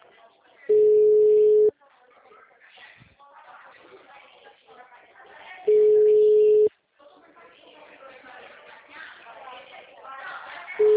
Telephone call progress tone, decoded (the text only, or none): ringback tone